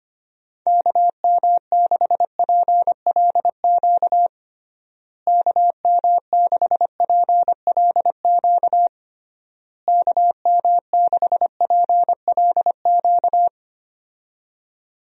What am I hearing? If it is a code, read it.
Morse code, heard as a keyed tone, about 25 wpm, 700 Hz. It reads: KM6PLQ KM6PLQ KM6PLQ
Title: USA Call Signs - 6 Characters - Repeated 3x - 25wpm